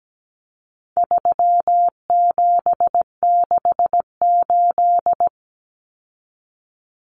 Morse code, heard as a keyed tone, about 17 wpm, 700 Hz. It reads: 3768
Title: Numbers - 4-Digit 17wpm